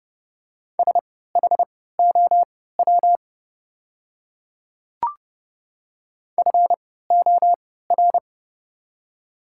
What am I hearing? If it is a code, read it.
Morse code, heard as a keyed tone, about 30 wpm, spaced as for 17 wpm, 700 Hz. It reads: SHOW FOR
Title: 3x Character Spacing - Top 200 Words - 30wpm